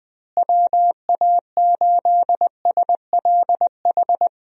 Morse code, heard as a keyed tone, about 20 wpm, 700 Hz. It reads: WA8SLH
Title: USA Call Signs - 6 Characters - Repeated 3x - 20wpm